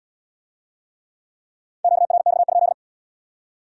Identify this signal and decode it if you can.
Morse code, heard as a keyed tone, about 72 wpm, 700 Hz. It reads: QRX?